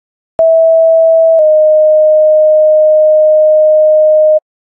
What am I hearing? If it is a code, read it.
Two-tone page, heard as a one-second tone, then a three-second tone, about 650 then 620 Hz. It is heard from 0.4 to 4.4 s.